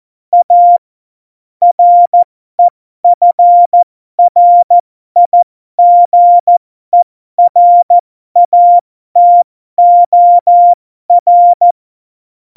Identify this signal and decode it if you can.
Morse code, heard as a keyed tone, about 14 wpm, spaced as for 12 wpm, 700 Hz. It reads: A REFRIGERATOR
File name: Morsekurs / Witze